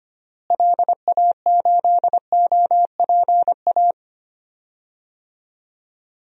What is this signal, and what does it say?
Morse code, heard as a keyed tone, about 25 wpm, 700 Hz. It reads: LA8OPA
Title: DXCC-100 Call Signs - 6 Characters - Repeated 2x - 25wpm